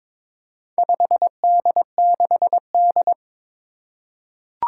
Morse code, heard as a keyed tone, about 22 wpm, 700 Hz. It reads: 5D6D
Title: Four Challenging Letter-Number Combinations 22wpm